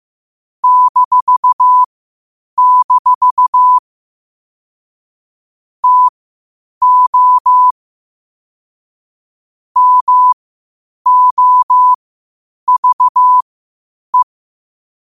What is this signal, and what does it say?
Morse code, heard as a keyed tone, about 15 wpm, spaced as for 8 wpm, 1000 Hz. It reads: -- TO MOVE
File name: qotdfiles / 8wpm